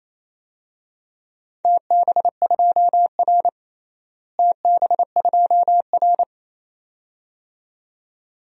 Morse code, heard as a keyed tone, about 28 wpm, 700 Hz. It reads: TB2R TB2R
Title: DXCC-100 Call Signs - 4 Characters - Repeated 2x - 28wpm